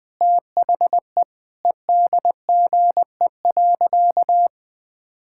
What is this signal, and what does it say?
Morse code, heard as a keyed tone, about 21 wpm, 700 Hz. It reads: THE EDGE.
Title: Sentences with 3-Word Phrases - 20wpm